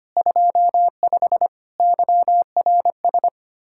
Morse code, heard as a keyed tone, about 25 wpm, 700 Hz. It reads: 25 YRS
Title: K1USN SST QSO Practice 25wpm